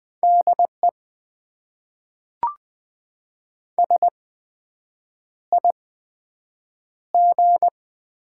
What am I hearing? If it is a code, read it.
Morse code, heard as a keyed tone, about 20 wpm, spaced as for 7 wpm, 700 Hz. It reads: DE SIG
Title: Top 90 QSO Elements - 8x Character Spacing - 20wpm